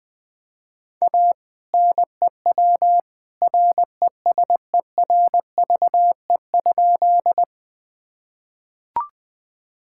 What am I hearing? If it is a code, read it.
Morse code, heard as a keyed tone, about 21 wpm, 700 Hz. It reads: A NEW RESERVE?
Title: Sentences from Top 1000 Words - Review Part 6 - 20wpm